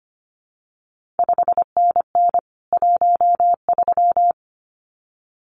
Morse code, heard as a keyed tone, about 25 wpm, 700 Hz. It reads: 5NN 13
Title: CQ WW DX Contest Exchange - 25wpm